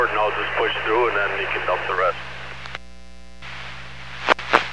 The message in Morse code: none

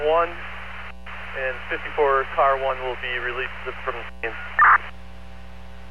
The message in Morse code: none

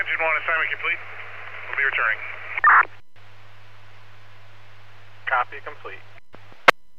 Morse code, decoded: none

